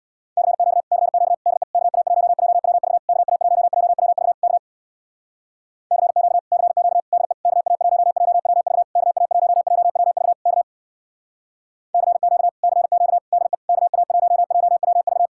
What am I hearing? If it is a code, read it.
Morse code, heard as a keyed tone, about 64 wpm, 700 Hz. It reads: CQ CQ DE KA1JWV KA1JWV K CQ CQ DE KA1JWV KA1JWV K CQ CQ DE KA1JWV